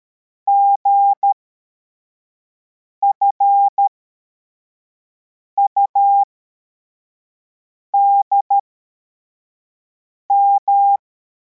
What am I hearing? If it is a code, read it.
Morse code, heard as a keyed tone, about 13 wpm, spaced as for 4 wpm, 800 Hz. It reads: GFUDM
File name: formation / cw / Text63